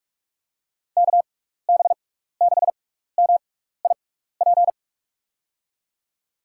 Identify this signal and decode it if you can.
Morse code, heard as a keyed tone, about 46 wpm, spaced as for 16 wpm, 700 Hz. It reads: KB6MIP
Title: USA Call Signs - 6x Character Spacing - 45wpm